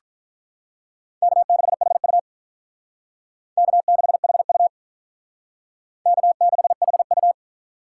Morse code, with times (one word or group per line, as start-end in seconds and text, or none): 1.2-2.2 K6HU
3.6-4.7 K6HU
6.1-7.3 K6HU